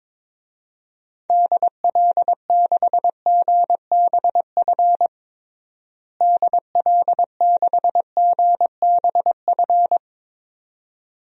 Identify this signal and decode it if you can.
Morse code, heard as a keyed tone, about 21 wpm, 700 Hz. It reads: DL6GBF DL6GBF
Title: DXCC-100 Call Signs - 6 Characters - Repeated 2x - 22wpm